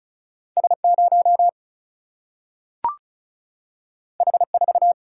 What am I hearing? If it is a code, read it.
Morse code, heard as a keyed tone, about 36 wpm, 700 Hz. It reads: S0 H4
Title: Single-Letter Single-Number 35wpm